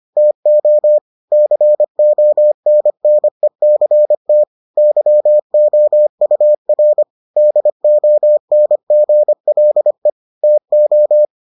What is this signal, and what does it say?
Morse code, heard as a keyed tone, about 25 wpm, 600 Hz. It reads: TO CONNECT YOUR DONGLE TO